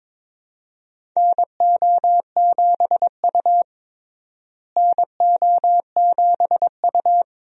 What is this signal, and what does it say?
Morse code, heard as a keyed tone, about 22 wpm, 700 Hz. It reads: NO7U NO7U
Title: USA Call Signs - 4 Characters - Repeated 2x - 22wpm